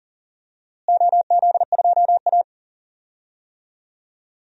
Morse code, heard as a keyed tone, about 37 wpm, 700 Hz. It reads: OZ2A
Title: DXCC-100 Call Signs - 4 Characters - Repeated 3x - 40wpm